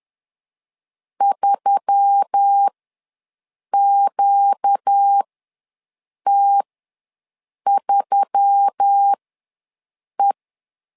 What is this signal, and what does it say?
Morse code, heard as a keyed tone, about 10 wpm, spaced as for 6 wpm, 800 Hz. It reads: 3QT3E
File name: LEZIONI / 4 Q1S3 ws 10wpm 2437